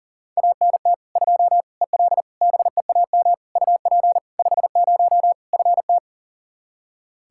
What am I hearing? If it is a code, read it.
Morse code, heard as a keyed tone, about 37 wpm, 700 Hz. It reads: ANT 2 EL BEAM UP 50 FT
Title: Practice QSO - Exchange 2 - Info - 40wpm